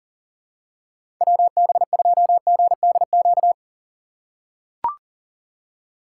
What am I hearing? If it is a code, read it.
Morse code, heard as a keyed tone, about 37 wpm, 700 Hz. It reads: WB2GDQ
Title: USA Call Signs - 6 Characters - 4x Character Spacing - 40wpm